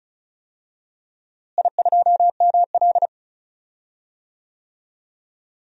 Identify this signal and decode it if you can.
Morse code, heard as a keyed tone, about 34 wpm, 700 Hz. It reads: I2ML
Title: DXCC-100 Call Signs - 4 Characters - Repeated 2x - 35wpm